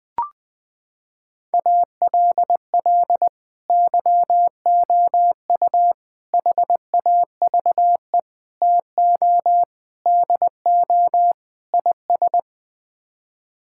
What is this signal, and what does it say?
Morse code, heard as a keyed tone, about 21 wpm, 700 Hz. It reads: ALL YOU HAVE TO DO IS
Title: Sets of 6 Words - 20wpm